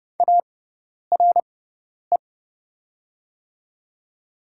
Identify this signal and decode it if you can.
Morse code, heard as a keyed tone, about 29 wpm, spaced as for 10 wpm, 700 Hz. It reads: ARE
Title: 6x Character Spacing - Top 300 Words - 30wpm